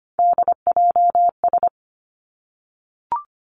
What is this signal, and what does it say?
Morse code, heard as a keyed tone, about 25 wpm, 700 Hz. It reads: DJS
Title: Three Challenging Letter-Number Combinations 25wpm